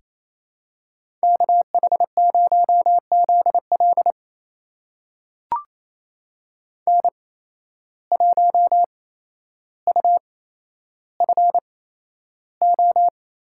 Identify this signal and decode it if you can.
Morse code, heard as a keyed tone, about 28 wpm, spaced as for 12 wpm, 700 Hz. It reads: KH0ZL N1UFO
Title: USA Call Signs - 8x Character Spacing - 28wpm